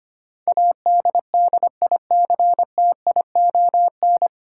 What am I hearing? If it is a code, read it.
Morse code, heard as a keyed tone, about 25 wpm, 700 Hz. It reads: ADDICTION
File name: Nine Letter Words - 25wpm